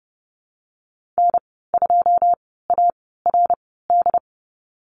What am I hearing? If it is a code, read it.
Morse code, heard as a keyed tone, about 30 wpm, spaced as for 17 wpm, 700 Hz. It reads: N2ARD
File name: Lesson 14g - Number 2 - Call Signs 30wpm